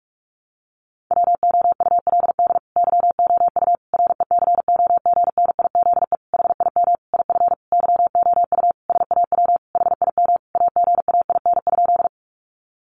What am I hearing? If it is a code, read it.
Morse code, heard as a keyed tone, about 44 wpm, 700 Hz. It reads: WOULD YOU RECOGNIZE HIM IF YOU SAW HIM AGAIN?
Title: Sentences from Top 1400 Words 45wpm